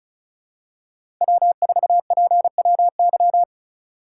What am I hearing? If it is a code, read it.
Morse code, heard as a keyed tone, about 36 wpm, 700 Hz. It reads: W4PWY
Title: USA Call Signs - 5 Characters - 3x Character Spacing - 35wpm